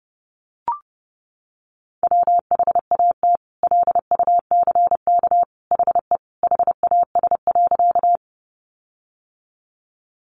Morse code, heard as a keyed tone, about 30 wpm, 700 Hz. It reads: WHAT LUCK HE HAS.
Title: Sentences from Top 1500 Words 30wpm